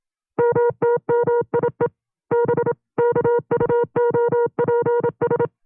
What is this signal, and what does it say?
Morse code, heard as a keyed tone, about 27 wpm, 480 Hz. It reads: MTMIE B KUOPS
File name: CW-practice/25WPM